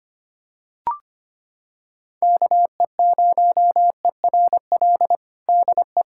Morse code, heard as a keyed tone, about 25 wpm, 700 Hz. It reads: KE0ERL DE
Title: USA Callsign DE Callsign 25wpm